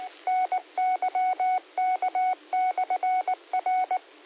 Morse code, heard as a keyed tone, about 19 wpm, 740 Hz. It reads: NYK/R